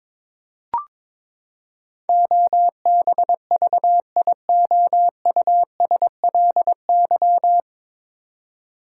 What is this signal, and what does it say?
Morse code, heard as a keyed tone, about 22 wpm, 700 Hz. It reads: OBVIOUSLY
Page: Morse Code Ninja practice 22wpm